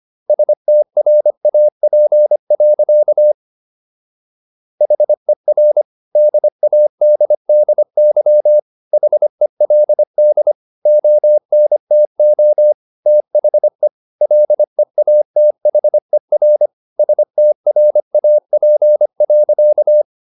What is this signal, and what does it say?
Morse code, heard as a keyed tone, about 25 wpm, 600 Hz. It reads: STRAP. HER DADDY HELD ONTO THE LEATHER STRAP.
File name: SS123 25